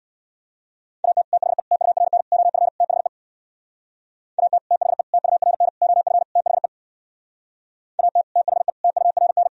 Morse code, heard as a keyed tone, about 64 wpm, 700 Hz. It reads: AT THE TURN OF THE AT THE TURN OF THE AT THE TURN